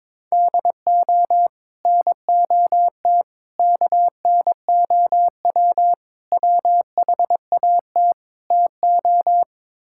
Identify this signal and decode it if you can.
Morse code, heard as a keyed tone, about 22 wpm, 700 Hz. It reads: DO NOT KNOW WHAT TO
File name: practice / 22wpm / Build A Phrase - 5-Word Phrases - 22wpm